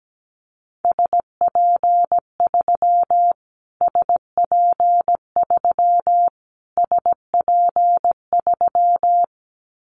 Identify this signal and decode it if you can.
Morse code, heard as a keyed tone, about 17 wpm, 700 Hz. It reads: SP3 SP3 SP3
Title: Callsign Prefixes for Top 25 DXCC - Repeated 3x - 17wpm